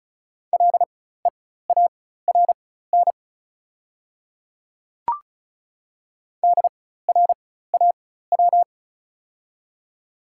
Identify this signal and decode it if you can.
Morse code, heard as a keyed tone, about 36 wpm, spaced as for 16 wpm, 690 Hz. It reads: LEARN DRAW
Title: 4x Character Spacing - Top 500 Words - 35wpm